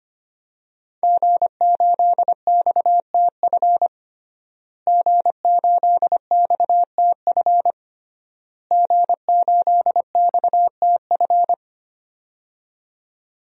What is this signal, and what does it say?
Morse code, heard as a keyed tone, about 25 wpm, 700 Hz. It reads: G8XTF G8XTF G8XTF
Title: DXCC-100 Call Signs - 5 Characters - Repeated 3x - 25wpm